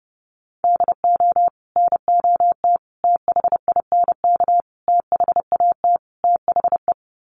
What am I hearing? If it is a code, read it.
Morse code, heard as a keyed tone, about 30 wpm, 700 Hz. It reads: DO NOT THINK THAT THE
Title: Sets of 5 Words Encore2 30wpm